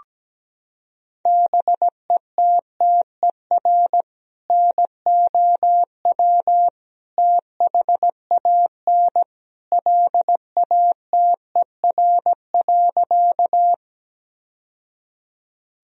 Morse code, heard as a keyed tone, about 17 wpm, 700 Hz. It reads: BETTER NOW THAN LATER.
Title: Sentences from Top 1000 Words - Review Part 4 - 17wpm